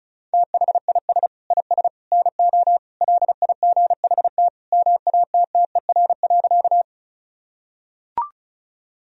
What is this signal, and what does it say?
Morse code, heard as a keyed tone, about 34 wpm, 700 Hz. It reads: THIS IS NO LIGHT MATTER.
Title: Sentences from Top 1000 Words - Review Part 1 - 35wpm